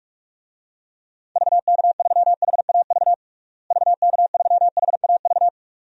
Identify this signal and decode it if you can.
Morse code, heard as a keyed tone, about 41 wpm, 700 Hz. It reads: VK3HAV VK3HAV
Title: DXCC-100 Call Signs - 6 Characters - Repeated 2x - 45wpm